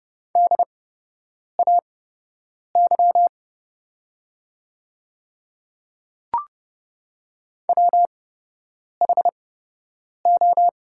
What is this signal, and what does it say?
Morse code, heard as a keyed tone, about 31 wpm, spaced as for 8 wpm, 700 Hz. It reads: DAY WHO
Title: 8x Character Spacing - Top 100 Words - 30wpm